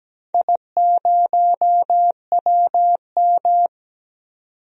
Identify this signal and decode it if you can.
Morse code, heard as a keyed tone, about 17 wpm, 700 Hz. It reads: I0WM
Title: DXCC-100 Call Signs - 4 Characters - 17wpm